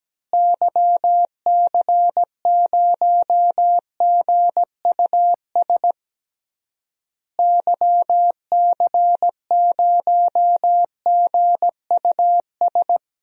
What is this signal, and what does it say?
Morse code, heard as a keyed tone, about 17 wpm, 700 Hz. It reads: YC0GUS YC0GUS
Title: DXCC-100 Call Signs - 6 Characters - Repeated 3x - 17wpm